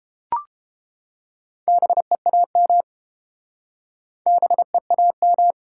Morse code, heard as a keyed tone, about 31 wpm, 700 Hz. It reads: BEAM BEAM